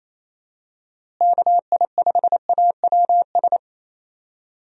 Morse code, heard as a keyed tone, about 28 wpm, 700 Hz. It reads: KI5AWS
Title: USA Call Signs - 6 Characters - Repeated 2x - 28wpm